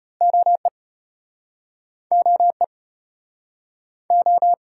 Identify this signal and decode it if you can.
Morse code, heard as a keyed tone, about 34 wpm, 700 Hz. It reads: OE OE O